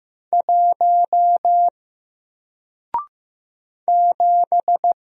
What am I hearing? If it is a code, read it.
Morse code, heard as a keyed tone, about 15 wpm, 700 Hz. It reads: 1 7
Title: Single Character (Letter:Number:Punctuation:Prosign) 15wpm